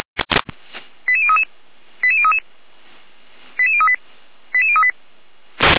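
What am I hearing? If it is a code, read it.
Five-tone selective call (ZVEI), heard as quick runs of five tones, tones of about 70 ms. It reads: 88033 88033 88038 88038
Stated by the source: ZVEI five-tone call received via SDR, archived by Feuerwehr Orlamünde (fire brigade).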